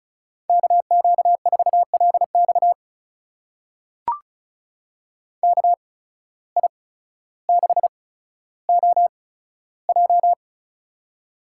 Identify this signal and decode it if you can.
Morse code, heard as a keyed tone, about 36 wpm, spaced as for 15 wpm, 700 Hz. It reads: KQ4LX KI6OJ
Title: USA Call Signs - 5 Characters - 8x Character Spacing - 35wpm